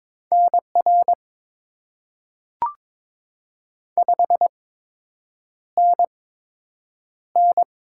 Morse code, heard as a keyed tone, about 22 wpm, spaced as for 8 wpm, 700 Hz. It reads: NR 5NN
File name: Top 90 QSO Elements - 8x Character Spacing - 22wpm